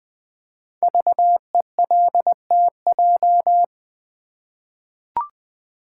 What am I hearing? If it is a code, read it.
Morse code, heard as a keyed tone, about 20 wpm, 700 Hz. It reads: VELTJ
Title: Code Groups - 5 Letter:Numbers - 20wpm